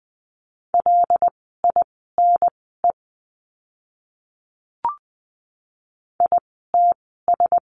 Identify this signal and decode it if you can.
Morse code, heard as a keyed tone, about 20 wpm, spaced as for 15 wpm, 700 Hz. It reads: LINE ITS
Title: 094 Lesson 16f - Letter c - Words 20wpm